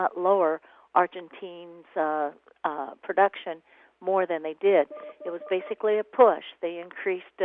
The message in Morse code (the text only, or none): none